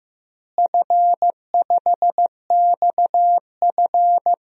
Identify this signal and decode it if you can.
Morse code, heard as a keyed tone, about 15 wpm, 700 Hz. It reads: F5XF
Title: DXCC-100 Call Signs - 4 Characters - Repeated 3x - 15wpm